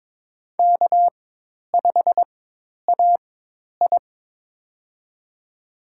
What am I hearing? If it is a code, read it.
Morse code, heard as a keyed tone, about 22 wpm, spaced as for 10 wpm, 700 Hz. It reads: K5AI
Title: USA Call Signs - 4 Characters - 4x Character Spacing - 22wpm